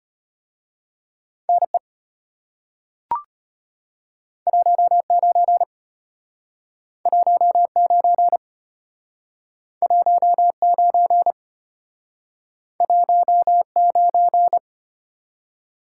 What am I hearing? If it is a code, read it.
Morse code, heard as a keyed tone, about 32 wpm, 700 Hz. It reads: NE 19 19 19 19